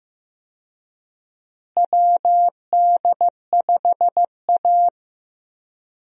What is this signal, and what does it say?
Morse code, heard as a keyed tone, about 15 wpm, 700 Hz. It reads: WD5A
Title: USA Call Signs - 4 Characters - Repeated 3x - 15wpm